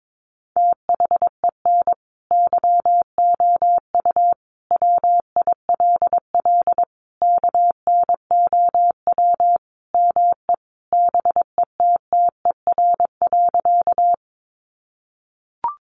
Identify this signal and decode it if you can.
Morse code, heard as a keyed tone, about 22 wpm, 700 Hz. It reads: THEN YOU WILL KNOW ME BETTER.
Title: Sentences from Top 400 Words 22wpm